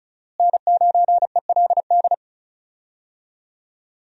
Morse code, heard as a keyed tone, about 36 wpm, 700 Hz. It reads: N9ELD
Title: USA Call Signs - 5 Characters - Repeated 2x - 35wpm